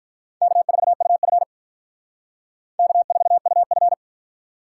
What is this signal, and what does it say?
Morse code, heard as a keyed tone, about 53 wpm, 700 Hz. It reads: K4UF K4UF